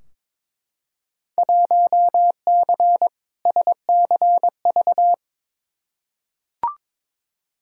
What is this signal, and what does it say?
Morse code, heard as a keyed tone, about 22 wpm, 700 Hz. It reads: 1C SCV
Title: Field Day Exchanges - 22wpm